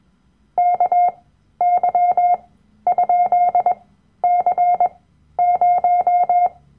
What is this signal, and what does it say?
Morse code, heard as a keyed tone, about 21 wpm, spaced as for 12 wpm, 710 Hz. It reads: KY?C0
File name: Workshop Dateien / CW-Online-Kurs / Lektion 010